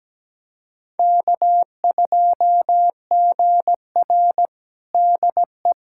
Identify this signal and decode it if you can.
Morse code, heard as a keyed tone, about 17 wpm, 700 Hz. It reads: K2GR DE
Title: USA Callsign DE Callsign - 4 Characters - 17wpm